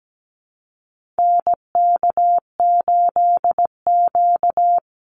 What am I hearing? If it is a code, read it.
Morse code, heard as a keyed tone, about 17 wpm, 700 Hz. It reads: NK8Q
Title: NA SOTA Activator Callsigns 17wpm